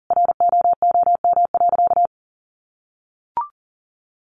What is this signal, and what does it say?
Morse code, heard as a keyed tone, about 40 wpm, 700 Hz. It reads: ROOM.
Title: Sentences from Top 300 Words 40wpm